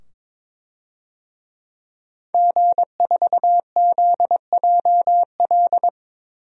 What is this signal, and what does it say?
Morse code, heard as a keyed tone, about 21 wpm, 700 Hz. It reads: G4ZJL
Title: DXCC-100 Call Signs - 5 Characters - 22wpm